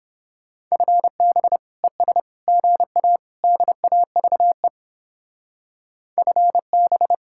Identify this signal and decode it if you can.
Morse code, heard as a keyed tone, about 31 wpm, 700 Hz. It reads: FB ES GA DAVE FB